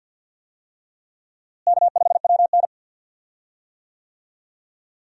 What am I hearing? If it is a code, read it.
Morse code, heard as a keyed tone, about 52 wpm, 700 Hz. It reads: K5WN
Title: USA Call Signs - 4 Characters - Repeated 3x - 50wpm